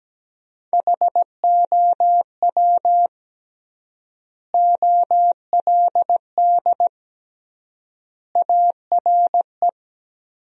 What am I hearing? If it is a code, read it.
Morse code, heard as a keyed tone, about 17 wpm, 700 Hz. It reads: HOW OLD ARE
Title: 3x Word-Spacing - Sentences from Top 200 Words - 17wpm